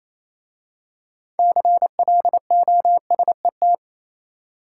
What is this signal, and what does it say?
Morse code, heard as a keyed tone, about 27 wpm, 700 Hz. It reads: CLOSET
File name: Easy Words - Speed-Racing with Hints - 22wpm